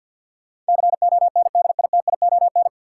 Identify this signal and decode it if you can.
Morse code, heard as a keyed tone, about 47 wpm, 700 Hz. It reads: CONDITION
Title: Sentences from Top 1000 Words - New Word Drill - 50wpm